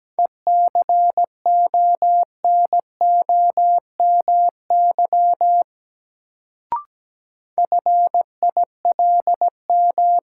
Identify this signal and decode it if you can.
Morse code, heard as a keyed tone, about 17 wpm, 700 Hz. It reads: ECONOMY FILM
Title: Sentences from Top 1000 Words - Review Part 3 - New Word Drill - 17wpm